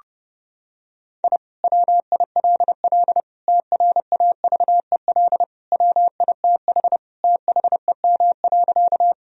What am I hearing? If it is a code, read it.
Morse code, heard as a keyed tone, about 30 wpm, 700 Hz. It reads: I WILL TRAVEL WITH THEM.
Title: Sentences from Top 1000 Words - Review Part 2 - 30wpm